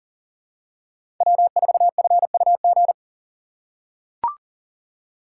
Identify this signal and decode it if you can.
Morse code, heard as a keyed tone, about 39 wpm, 700 Hz. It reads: W4FUG
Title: USA Call Signs - 2x Character Spacing - 40wpm